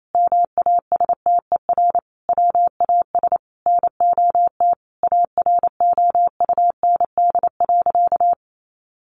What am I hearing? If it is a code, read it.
Morse code, heard as a keyed tone, about 28 wpm, 700 Hz. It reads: MASTER WAS NOT AROUND.
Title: Sentences from Top 1300 Words 28wpm